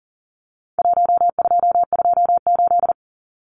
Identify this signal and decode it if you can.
Morse code, heard as a keyed tone, about 40 wpm, 700 Hz. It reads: 1228
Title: Numbers - Contest Style - 4-Digit 40wpm